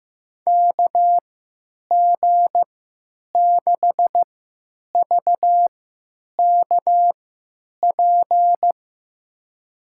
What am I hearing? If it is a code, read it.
Morse code, heard as a keyed tone, about 15 wpm, spaced as for 9 wpm, 700 Hz. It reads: KG6VKP
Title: USA Call Signs - 6 Characters - 3x Character Spacing - 15wpm